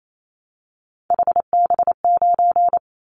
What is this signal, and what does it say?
Morse code, heard as a keyed tone, about 28 wpm, 700 Hz. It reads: HB9
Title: Callsign Prefixes for Top 25 DXCC - 28wpm